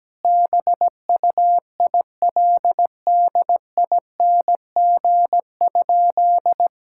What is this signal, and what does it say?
Morse code, heard as a keyed tone, about 17 wpm, 700 Hz. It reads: BUILDING?
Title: Sentences from Top 1000 Words - Review Part 6 - 17wpm